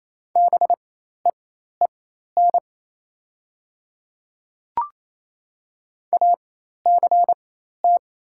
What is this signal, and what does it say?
Morse code, heard as a keyed tone, about 27 wpm, spaced as for 13 wpm, 690 Hz. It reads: BEEN ACT
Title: 4x Character Spacing - Top 300 Words - 28wpm